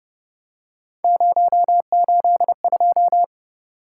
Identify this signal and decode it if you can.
Morse code, heard as a keyed tone, about 31 wpm, 700 Hz. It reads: 082
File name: Speed-Racing - Three Letter-Number Combinations - 25wpm